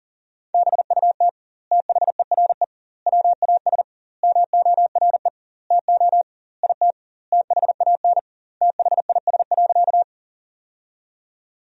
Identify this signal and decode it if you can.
Morse code, heard as a keyed tone, about 40 wpm, 700 Hz. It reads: BUT THERE WAS MORE TO IT THAN THIS.
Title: 2x Word-Spacing - Sentences from Top 100 Words - 40wpm